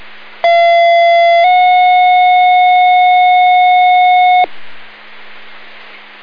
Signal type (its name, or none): two-tone page